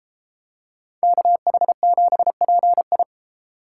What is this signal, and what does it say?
Morse code, heard as a keyed tone, about 32 wpm, 700 Hz. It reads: KH7PI